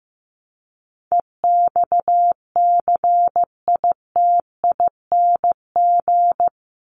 Morse code, heard as a keyed tone, about 15 wpm, 700 Hz. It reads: EXCITING
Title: Top 1751-2000 Words 15wpm